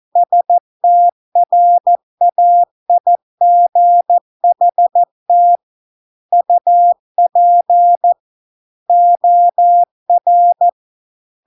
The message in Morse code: STRAIGHT UP OR

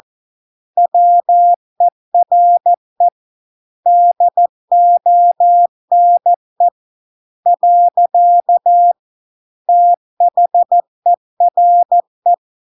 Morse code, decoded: WERE DONE . THERE

